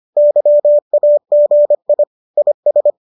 Morse code, heard as a keyed tone, about 25 wpm, 590 Hz. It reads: YAGI IS